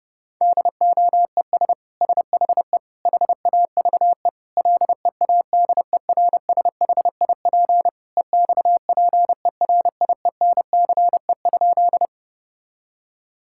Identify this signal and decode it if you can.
Morse code, heard as a keyed tone, about 30 wpm, 700 Hz. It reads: DOES SHE HAVE LEADERSHIP EXPERIENCE?
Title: Sentences from Top 1600 Words 30wpm